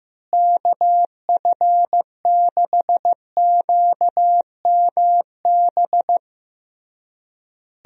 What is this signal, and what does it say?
Morse code, heard as a keyed tone, about 15 wpm, 700 Hz. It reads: KF6QMB
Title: USA Call Signs - 6 Characters - Repeated 3x - 15wpm